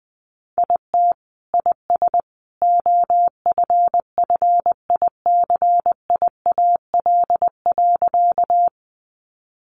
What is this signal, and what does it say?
Morse code, heard as a keyed tone, about 20 wpm, 700 Hz. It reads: IT IS OFFICIAL.
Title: Sentences from Top 1000 Words 20wpm